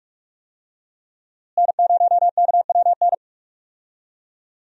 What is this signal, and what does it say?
Morse code, heard as a keyed tone, about 45 wpm, 700 Hz. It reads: N0KWN
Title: USA Call Signs - 5 Characters - Repeated 2x - 45wpm